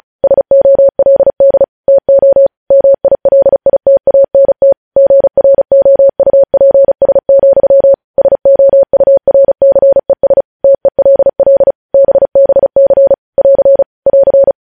Morse code, heard as a keyed tone, about 36 wpm, 550 Hz. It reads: SOLD TO MILITANT GROUPS, SOURCES TELL BBC <AR> <AR>
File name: podcast / News headlines-1x-35x35